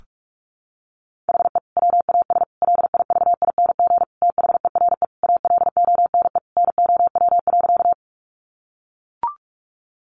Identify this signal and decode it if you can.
Morse code, heard as a keyed tone, about 44 wpm, 700 Hz. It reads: HE WAS LIVING THERE ALONE NOW.
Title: Sentences from Top 1100 Words 45wpm